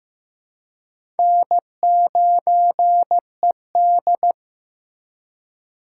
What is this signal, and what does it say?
Morse code, heard as a keyed tone, about 15 wpm, 700 Hz. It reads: N9ED
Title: USA Call Signs - 4 Characters - 3x Character Spacing - 15wpm